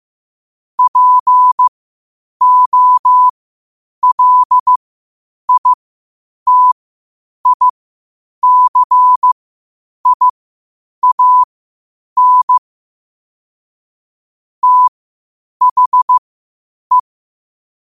Morse code, heard as a keyed tone, about 15 wpm, spaced as for 8 wpm, 1000 Hz. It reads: POLITICIAN THE
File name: qotdfiles / 8wpm